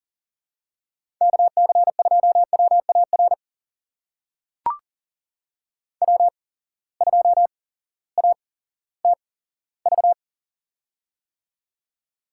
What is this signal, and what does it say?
Morse code, heard as a keyed tone, about 40 wpm, spaced as for 18 wpm, 700 Hz. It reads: KC2WAR W2ATV
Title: USA Call Signs - 8x Character Spacing - 40wpm